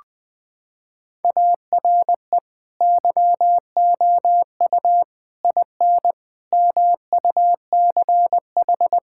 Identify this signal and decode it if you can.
Morse code, heard as a keyed tone, about 20 wpm, 700 Hz. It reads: ARE YOU IN MUCH
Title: Sentences from Top 1600 Words 20wpm